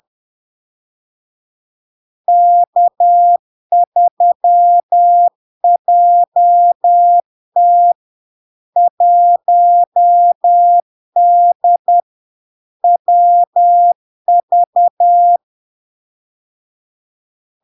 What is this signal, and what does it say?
Morse code, heard as a keyed tone, about 10 wpm, 700 Hz. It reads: K3JT 1D WV
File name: LARK-FD-2022-Log-in-CW-10wpm-700-Hz